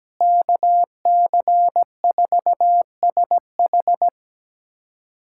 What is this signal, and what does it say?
Morse code, heard as a keyed tone, about 17 wpm, 700 Hz. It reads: KC4SH
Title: USA Callsign DE Callsign - 5 Characters - 17wpm